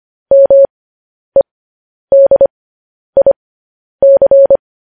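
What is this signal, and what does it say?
Morse code, heard as a keyed tone, about 25 wpm, spaced as for 10 wpm, 550 Hz. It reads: MEDIC
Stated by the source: Morse code news headlines sent at 10 words per minute.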